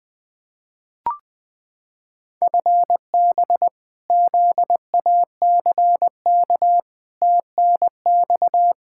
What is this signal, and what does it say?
Morse code, heard as a keyed tone, about 20 wpm, 700 Hz. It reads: FB ZACK TNX
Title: Practice QSO - Exchange 3 - More Info - 20wpm